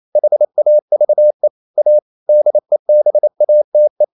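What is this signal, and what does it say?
Morse code, heard as a keyed tone, about 28 wpm, 600 Hz. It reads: HAVE A DEBATE
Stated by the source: Morse code practice at 28 wpm, text of Ken Follett's novel Never in English, from CW med Gnister.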